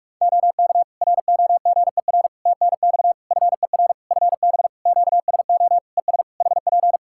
Morse code, heard as a keyed tone, about 46 wpm, 700 Hz. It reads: OK ROGER TNX FER FB QSO ES HP
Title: Practice QSO - Exchange 4 - Ending - 45wpm